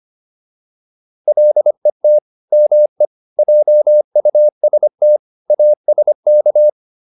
Morse code, heard as a keyed tone, about 25 wpm, 600 Hz. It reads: LET ME JUST ASK